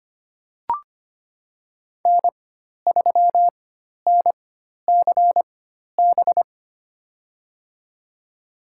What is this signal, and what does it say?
Morse code, heard as a keyed tone, about 25 wpm, spaced as for 12 wpm, 700 Hz. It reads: N3NCB
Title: USA Call Signs - 5 Characters - 4x Character Spacing - 25wpm